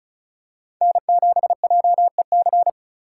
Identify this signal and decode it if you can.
Morse code, heard as a keyed tone, about 34 wpm, 700 Hz. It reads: N7JEC
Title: USA Call Signs - 5 Characters - 2x Character Spacing - 35wpm